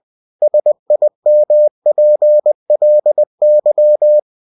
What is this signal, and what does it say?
Morse code, heard as a keyed tone, about 20 wpm, 600 Hz. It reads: SIMPLY
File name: cw-books / winnie-the-pooh / winnie-20wpm-page-0041